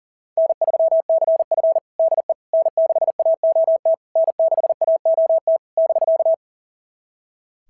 Morse code, heard as a keyed tone, about 38 wpm, 660 Hz. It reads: N3CF DE N6AOT N6AOT <BK>
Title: SOTA QSO Practice 40wpm